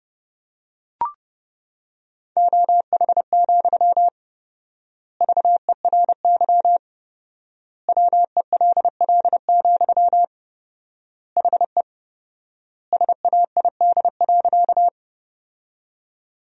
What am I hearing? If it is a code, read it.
Morse code, heard as a keyed tone, about 30 wpm, 700 Hz. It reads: OH, VERY WELL, HE SAID.
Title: 4x Word-Spacing - Sentences from Top 500 Words - 30wpm